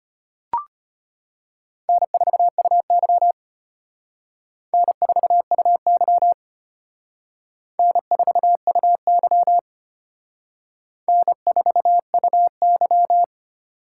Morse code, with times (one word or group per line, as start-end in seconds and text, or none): 1.9-3.3 N4UY
4.7-6.3 N4UY
7.8-9.6 N4UY
11.1-13.2 N4UY